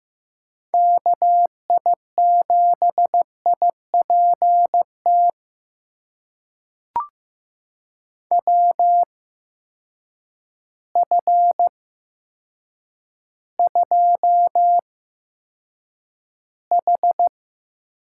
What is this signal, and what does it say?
Morse code, heard as a keyed tone, about 15 wpm, spaced as for 8 wpm, 700 Hz. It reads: KI7IPT WF2H
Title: USA Call Signs - 8x Character Spacing - 15wpm